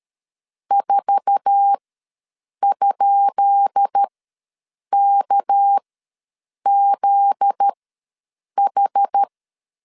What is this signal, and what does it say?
Morse code, heard as a keyed tone, about 13 wpm, spaced as for 7 wpm, 800 Hz. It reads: 4?KZH
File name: LEZIONI / 10 SIGNS / 10B INT ws 12wpm 3527